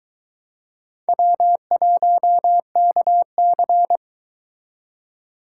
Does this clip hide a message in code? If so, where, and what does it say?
Yes: Morse code from 1.1 to 4.0 s, heard as a keyed tone, about 23 wpm, 700 Hz. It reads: W1KC